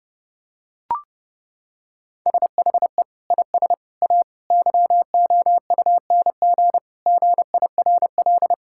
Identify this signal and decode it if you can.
Morse code, heard as a keyed tone, about 32 wpm, 700 Hz. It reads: SHE IS A YOUNG GIRL
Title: Sentences from Top 1600 Words 30wpm